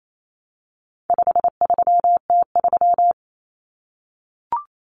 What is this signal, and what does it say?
Morse code, heard as a keyed tone, about 28 wpm, 700 Hz. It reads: 53T3
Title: Common Cut Numbers - 4-Digit 28wpm